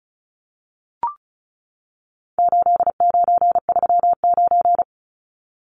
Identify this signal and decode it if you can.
Morse code, heard as a keyed tone, about 34 wpm, 700 Hz. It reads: 8939
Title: Numbers - 4-Digit 35wpm